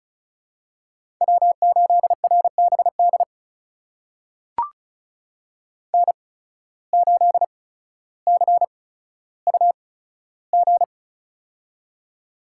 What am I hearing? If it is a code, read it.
Morse code, heard as a keyed tone, about 36 wpm, spaced as for 15 wpm, 700 Hz. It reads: W8RBD N8CUG
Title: USA Call Signs - 5 Characters - 8x Character Spacing - 35wpm